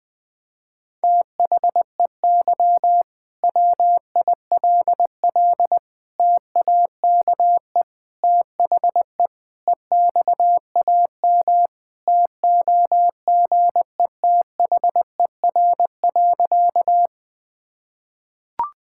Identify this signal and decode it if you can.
Morse code, heard as a keyed tone, about 20 wpm, 700 Hz. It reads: THEY WILL TAKE THE EXAM TOGETHER.
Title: Sentences from Top 1000 Words - Review Part 5 - 20wpm